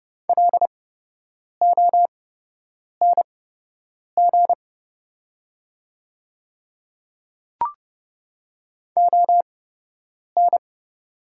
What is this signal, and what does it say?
Morse code, heard as a keyed tone, about 30 wpm, spaced as for 8 wpm, 700 Hz. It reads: LONG ON
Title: 8x Character Spacing - Top 200 Words - 30wpm